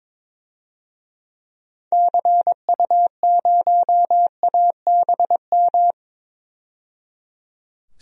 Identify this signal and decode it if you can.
Morse code, heard as a keyed tone, about 22 wpm, 700 Hz. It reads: CU0ABM
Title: DXCC-100 Call Signs - 6 Characters - 22wpm